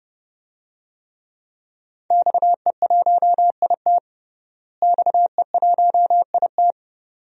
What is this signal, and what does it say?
Morse code, heard as a keyed tone, about 30 wpm, 700 Hz. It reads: XE1IT XE1IT
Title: DXCC-100 Call Signs - 5 Characters - Repeated 2x - 30wpm